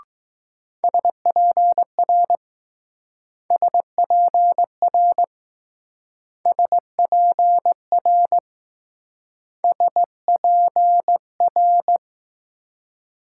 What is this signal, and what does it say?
Morse code, heard as a keyed tone, about 20 wpm, 700 Hz. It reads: SPR SPR SPR SPR